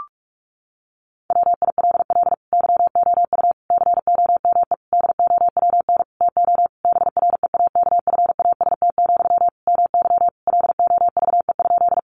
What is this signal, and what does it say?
Morse code, heard as a keyed tone, about 44 wpm, 700 Hz. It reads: WILL YOU COME DOWN TO BREAKFAST, MY LOVE?
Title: Sentences from Top 1500 Words 45wpm